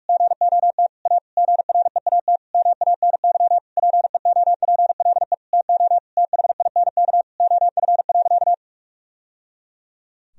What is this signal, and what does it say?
Morse code, heard as a keyed tone, about 46 wpm, 700 Hz. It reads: GOT A GREAT MANY PEOPLE TO THINK OF.
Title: Sentences from Top 1000 Words - Review Part 1 - 45wpm